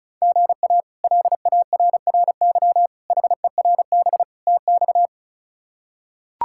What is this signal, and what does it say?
Morse code, heard as a keyed tone, about 34 wpm, 700 Hz. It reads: GA LARRY HERB TX
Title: K1USN SST QSO Practice - 35wpm